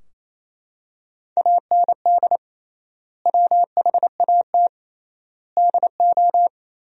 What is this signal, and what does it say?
Morse code, heard as a keyed tone, about 28 wpm, 700 Hz. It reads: AND WHAT DO
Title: 3x Word-Spacing - Sentences from Top 100 Words - 28wpm